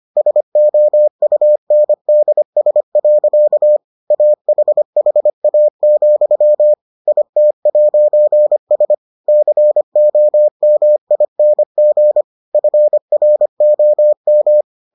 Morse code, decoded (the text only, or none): SOUNDS. AHHA, IT'S COMING FROM